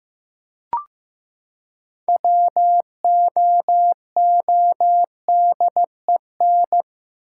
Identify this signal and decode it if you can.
Morse code, heard as a keyed tone, about 15 wpm, 700 Hz. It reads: WOODEN
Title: Six Letter Words - 15wpm